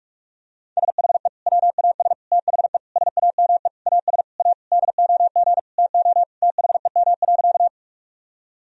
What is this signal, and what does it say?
Morse code, heard as a keyed tone, about 46 wpm, 700 Hz. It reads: SHE WAS THE SAME AS A DOG TO THEM.
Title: Sentences from Top 1000 Words - Review Part 2 - 45wpm